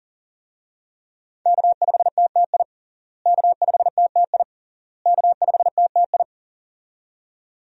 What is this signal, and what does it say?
Morse code, heard as a keyed tone, about 40 wpm, 700 Hz. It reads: K5TTI K5TTI K5TTI
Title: USA Call Signs - 5 Characters - Repeated 3x - 40wpm